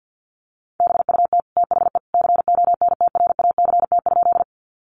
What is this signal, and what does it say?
Morse code, heard as a keyed tone, about 50 wpm, 700 Hz. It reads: BUT THE CONTRACT?
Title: Sentences from Top 1400 Words 50wpm